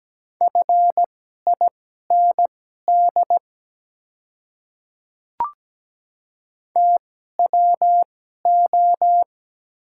Morse code, heard as a keyed tone, about 17 wpm, spaced as for 12 wpm, 700 Hz. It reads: FIND TWO
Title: 2x Character Spacing - Top 100 Words - 17wpm